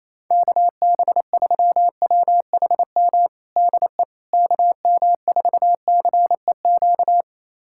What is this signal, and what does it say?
Morse code, heard as a keyed tone, about 27 wpm, 700 Hz. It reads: KB3WHM DE KM4CEQ
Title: USA Callsign DE Callsign - 6 Characters - 28wpm